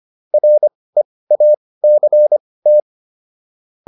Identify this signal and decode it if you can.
Morse code, heard as a keyed tone, about 25 wpm, spaced as for 18 wpm, 600 Hz. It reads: REACT